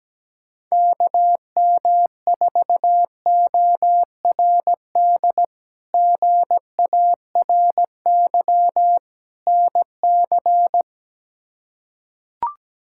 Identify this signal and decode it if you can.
Morse code, heard as a keyed tone, about 17 wpm, 700 Hz. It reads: KM4ORD GARY NC
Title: K1USN SST QSO Practice - 17wpm